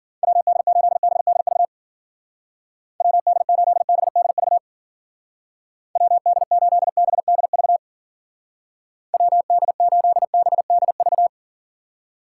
Morse code, heard as a keyed tone, about 48 wpm, 700 Hz. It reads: WD8BDV WD8BDV WD8BDV WD8BDV
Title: Speed Racing - USA 2x3 Call Signs 40wpm